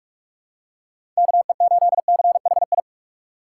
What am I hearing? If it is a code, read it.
Morse code, heard as a keyed tone, about 46 wpm, 700 Hz. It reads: K E 8 C H I
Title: USA Call Signs - 6 Characters - 6x Character Spacing - 45wpm